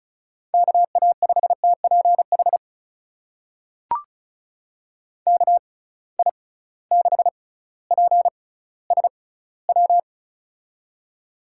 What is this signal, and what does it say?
Morse code, heard as a keyed tone, about 36 wpm, spaced as for 18 wpm, 690 Hz. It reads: KA5TPH KI6PSW
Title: USA Call Signs - 6 Characters - 6x Character Spacing - 35wpm